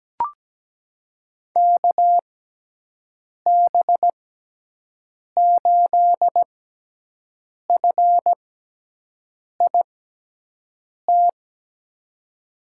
Morse code, heard as a keyed tone, about 17 wpm, spaced as for 6 wpm, 700 Hz. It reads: KB8FIT
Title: USA Call Signs - 6 Characters - 6x Character Spacing - 17wpm